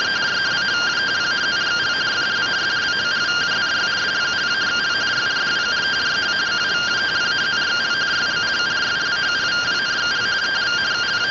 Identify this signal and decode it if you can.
Two-tone teleprinter signal, about 170 Hz shift.